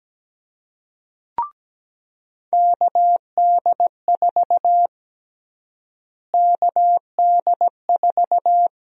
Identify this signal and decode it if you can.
Morse code, heard as a keyed tone, about 17 wpm, 700 Hz. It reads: KD4 KD4
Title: USA Call Sign Prefixes - Repeated 2x - 17wpm